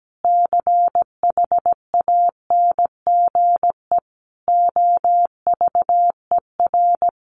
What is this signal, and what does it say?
Morse code, heard as a keyed tone, about 17 wpm, 700 Hz. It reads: CHANGE OVER
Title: Sentences from Top 1500 Words 17wpm